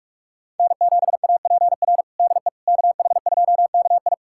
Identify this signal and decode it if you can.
Morse code, heard as a keyed tone, about 46 wpm, 700 Hz. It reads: N7APR DE KH2KI
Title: USA Callsign DE Callsign - 5 Characters - 45wpm